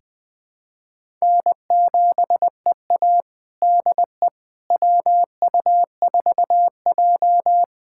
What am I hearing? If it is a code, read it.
Morse code, heard as a keyed tone, about 20 wpm, 700 Hz. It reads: N7EA DE WU4J
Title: USA Callsign DE Callsign - 4 Characters - 20wpm